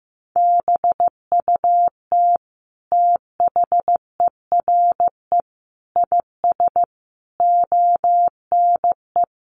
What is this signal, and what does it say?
Morse code, heard as a keyed tone, about 15 wpm, 700 Hz. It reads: BUT THERE IS ONE
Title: Sentences from Top 1200 Words 15wpm